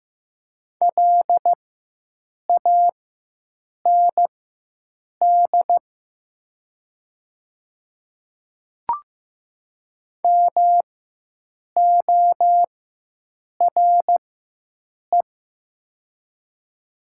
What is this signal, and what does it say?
Morse code, heard as a keyed tone, about 15 wpm, spaced as for 7 wpm, 700 Hz. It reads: LAND MORE
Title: 4x Character Spacing - Top 200 Words - 15wpm